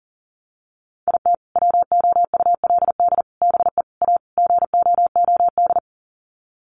Morse code, heard as a keyed tone, about 40 wpm, 700 Hz. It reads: IT WOULD BE A GOOD
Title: Sets of 5 Words Encore2 40wpm